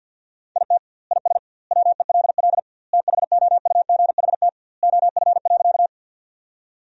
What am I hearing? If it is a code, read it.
Morse code, heard as a keyed tone, about 50 wpm, 700 Hz. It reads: IT IS WELL THOUGHT OF.
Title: 2x Word-Spacing - Sentences from Top 300 Words - 50wpm